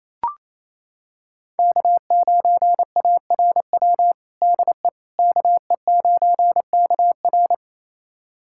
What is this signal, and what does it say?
Morse code, heard as a keyed tone, about 27 wpm, 690 Hz. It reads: K9ARW DE KE9KR
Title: USA Callsign DE Callsign - 5 Characters - 28wpm